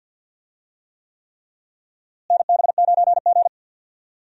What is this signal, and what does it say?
Morse code, heard as a keyed tone, about 49 wpm, 700 Hz. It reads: NB9G
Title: USA Call Signs - 4 Characters - Repeated 3x - 50wpm